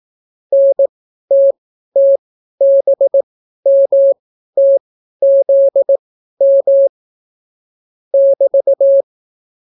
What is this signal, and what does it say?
Morse code, heard as a keyed tone, about 18 wpm, spaced as for 12 wpm, 550 Hz. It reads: NTTBMTZM <BT>